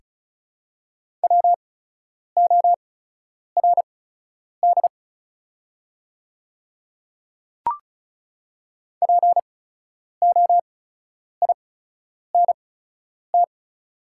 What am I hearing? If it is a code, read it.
Morse code, heard as a keyed tone, about 36 wpm, spaced as for 10 wpm, 700 Hz. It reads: WORD POINT
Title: 8x Character Spacing - Top 200 Words - 35wpm